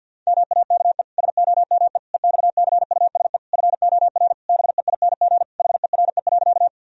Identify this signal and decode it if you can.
Morse code, heard as a keyed tone, about 50 wpm, 700 Hz. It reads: MAKE SOME EXCUSE FOR BEING HERE.
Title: Sentences from Top 1000 Words - Review Part 3 - 50wpm